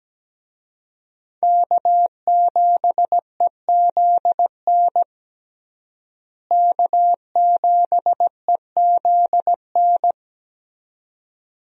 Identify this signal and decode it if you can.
Morse code, heard as a keyed tone, about 17 wpm, 700 Hz. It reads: K7EZN K7EZN
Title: USA Call Signs - 5 Characters - Repeated 2x - 17wpm